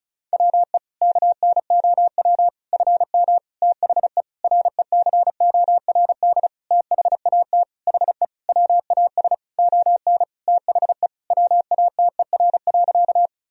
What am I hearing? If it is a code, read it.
Morse code, heard as a keyed tone, about 36 wpm, 700 Hz. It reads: WE KNOW FM THE RECORD THAT HE WAS ON THE WATER.
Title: Sentences from Top 500 Words and Abbreviations 35wpm